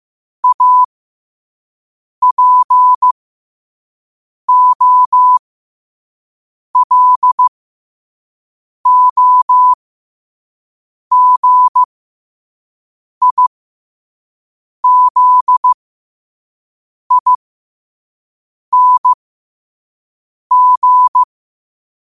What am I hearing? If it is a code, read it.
Morse code, heard as a keyed tone, about 15 wpm, spaced as for 5 wpm, 1000 Hz. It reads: APOLOGIZING
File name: qotdfiles / 5wpm